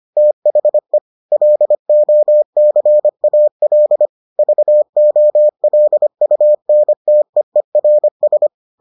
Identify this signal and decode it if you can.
Morse code, heard as a keyed tone, about 25 wpm, 600 Hz. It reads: THE LOCAL VOLUNTEERS